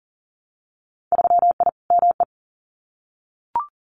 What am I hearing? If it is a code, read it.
Morse code, heard as a keyed tone, about 38 wpm, 700 Hz. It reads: 3I ME
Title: Winter Field Day Exchanges 40wpm